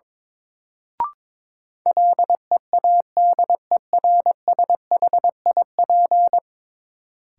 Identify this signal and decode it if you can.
Morse code, heard as a keyed tone, about 22 wpm, 700 Hz. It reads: LEADERSHIP
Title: Sentences from Top 1600 Words - New Word Drill - 22wpm